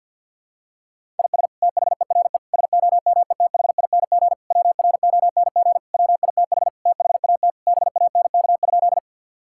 Morse code, heard as a keyed tone, about 52 wpm, 700 Hz. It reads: IS THERE SOMETHING WRONG WITH THAT BANK?
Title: Sentences from Top 1000 Words - Review Part 6 - 50wpm